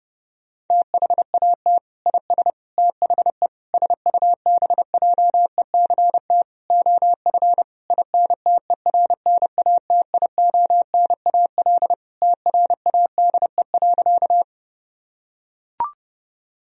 Morse code, heard as a keyed tone, about 30 wpm, 700 Hz. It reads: THAT IS THE SUBJECT OF INTERNATIONAL TRADE.
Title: Sentences from Top 1000 Words - Review Part 4 - 30wpm